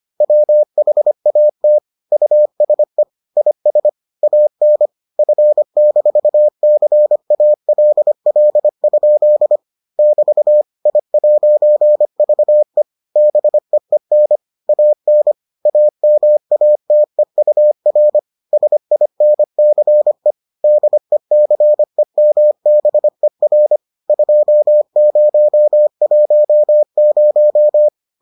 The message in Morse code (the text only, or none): WHAT USE IS AN F-CALL? <BT> I'VE BEEN AN AMATEUR SINCE DECEMBER 2010